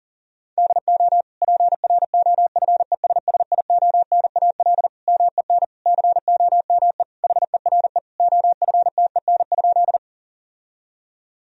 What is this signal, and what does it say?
Morse code, heard as a keyed tone, about 40 wpm, 700 Hz. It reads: DO PROFESSIONAL MEN COME HERE OFTEN?
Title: Sentences from Top 1000 Words - Review Part 4 - 40wpm